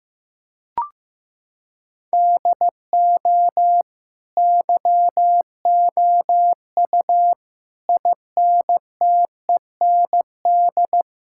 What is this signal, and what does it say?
Morse code, heard as a keyed tone, about 15 wpm, 700 Hz. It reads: DO YOU INTEND
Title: Sentences from Top 1600 Words 15wpm